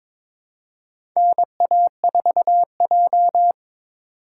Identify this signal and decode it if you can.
Morse code, heard as a keyed tone, about 22 wpm, 700 Hz. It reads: NA4J
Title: USA Call Signs - 4 Characters - 3x Character Spacing - 22wpm